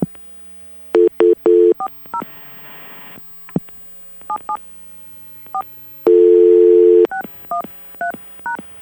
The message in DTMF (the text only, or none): 40 77 4 613#